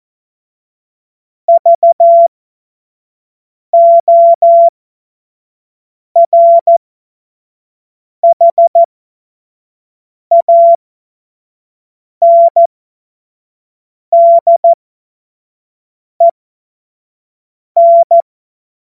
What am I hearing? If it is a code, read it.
Morse code, heard as a keyed tone, about 14 wpm, spaced as for 5 wpm, 680 Hz. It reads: VORHANDEN
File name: Morsekurs / Witze / Witz 02. August 2022 25 F70